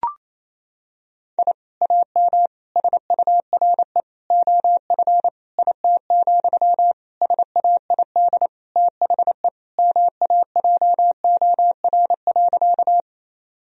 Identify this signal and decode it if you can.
Morse code, heard as a keyed tone, about 28 wpm, 700 Hz. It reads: I AM SURE OF IT, SAID THE MAJOR.
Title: Sentences from Top 1000 Words - Review Part 4 - 28wpm